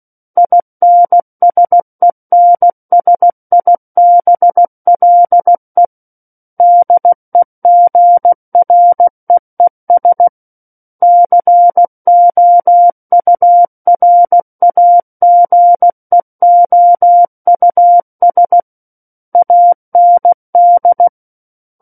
Morse code, heard as a keyed tone, about 16 wpm, 700 Hz. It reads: INSENSIBLE DEGREES COURAGEOUS AND
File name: Book One / 07-Chapter 7 - 16 WPM